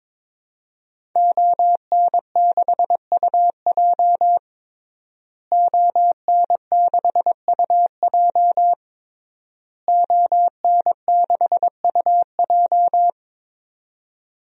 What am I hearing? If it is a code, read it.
Morse code, heard as a keyed tone, about 22 wpm, 700 Hz. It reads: ON6UJ ON6UJ ON6UJ
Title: DXCC-100 Call Signs - 5 Characters - Repeated 3x - 22wpm